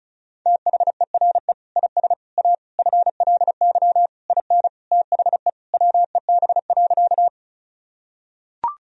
Morse code, heard as a keyed tone, about 36 wpm, 700 Hz. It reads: THERE IS A FLY IN THE WEB.
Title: Sentences from Top 1000 Words - Review Part 4 - 35wpm